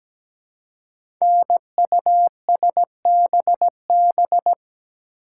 Morse code, heard as a keyed tone, about 17 wpm, 700 Hz. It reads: NUSBB
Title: Code Groups - 5 Letters - 17wpm